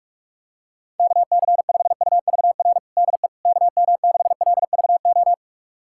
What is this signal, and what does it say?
Morse code, heard as a keyed tone, about 46 wpm, 700 Hz. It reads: KC5UVR DE KM6LVO
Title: USA Callsign DE Callsign 45wpm